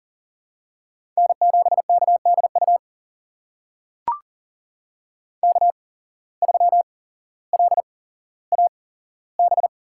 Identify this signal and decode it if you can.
Morse code, heard as a keyed tone, about 42 wpm, spaced as for 17 wpm, 700 Hz. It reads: N7KDU K3LAB
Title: USA Call Signs - 5 Characters - 8x Character Spacing - 40wpm